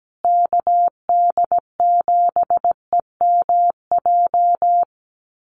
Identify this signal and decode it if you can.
Morse code, heard as a keyed tone, about 17 wpm, 700 Hz. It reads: KD7EMJ
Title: USA Call Signs Repeated 3x 17wpm